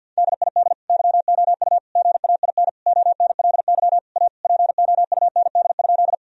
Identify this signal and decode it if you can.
Morse code, heard as a keyed tone, about 52 wpm, 700 Hz. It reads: DID YOU GAIN ONLY A POUND?
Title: Sentences from Top 1000 Words - Review Part 2 - 50wpm